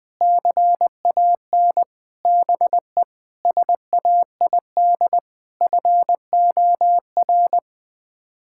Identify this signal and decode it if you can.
Morse code, heard as a keyed tone, about 20 wpm, 700 Hz. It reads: CAN BE SAID FOR
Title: Sets of 8 Words - 20wpm